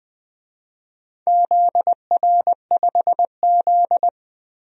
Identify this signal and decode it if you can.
Morse code, heard as a keyed tone, about 20 wpm, 700 Hz. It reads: ZR5Z
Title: DXCC-100 Call Signs - 4 Characters - 20wpm